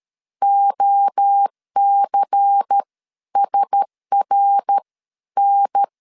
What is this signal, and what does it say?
Morse code, heard as a keyed tone, about 13 wpm, spaced as for 11 wpm, 800 Hz. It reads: OCSRN
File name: LEZIONI / 8 YJ45 / 8 Y 12wpm 2228